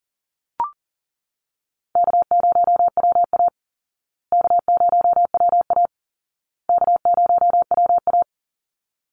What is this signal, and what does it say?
Morse code, heard as a keyed tone, about 40 wpm, 700 Hz. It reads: K0WA K0WA K0WA
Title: Easy Call Signs Repeated 3x - 40wpm